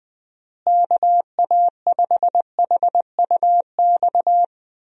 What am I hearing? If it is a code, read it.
Morse code, heard as a keyed tone, about 20 wpm, 700 Hz. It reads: KA5HUX
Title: USA Call Signs - 6 Characters - Repeated 2x - 20wpm